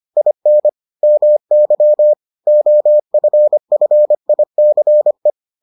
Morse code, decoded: IN MY OFFICE

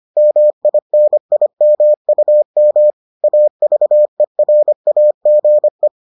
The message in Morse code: MINIMUM AVERAGE

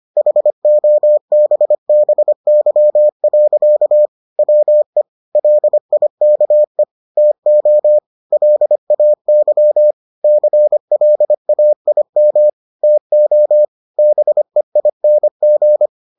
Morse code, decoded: HOBBY. WE LIKE TO LAY CLAIM TO BEING

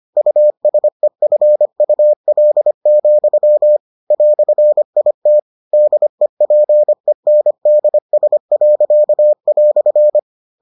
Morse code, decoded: USEFUL, "IT DEPENDS."

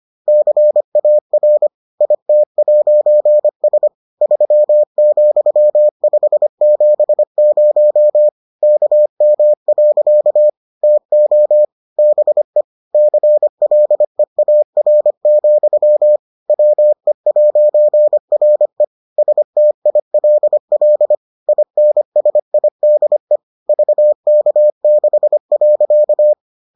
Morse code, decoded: CAR IT'S 3,570 KM. TO BE CLEAR, WE'RE STILL INSIDE VK6.

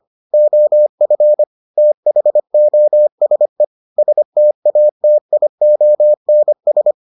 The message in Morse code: OF THOSE STATIONS

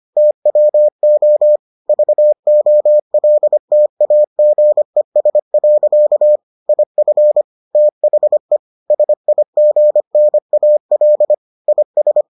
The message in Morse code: TWO VOLTAGES. IF THE SIGNAL IS

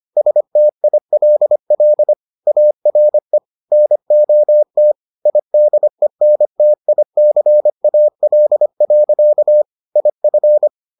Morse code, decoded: STILL ARE NOT IDENTICAL. IF

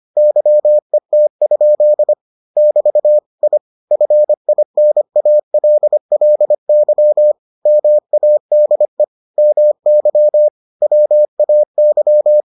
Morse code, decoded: YET? <BT> I FINALLY MADE MY WAY

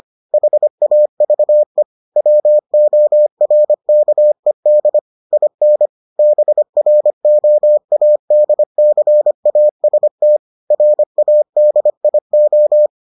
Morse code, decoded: HAVE WORKED IN BROADCAST RADIO

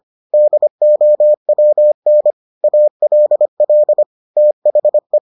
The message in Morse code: DOWN ALL THE

